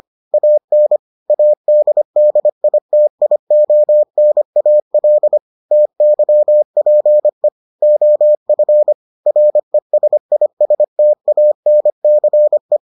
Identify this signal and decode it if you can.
Morse code, heard as a keyed tone, about 25 wpm, 600 Hz. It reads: AN ADDITIONAL TYPE OF RESISTANCE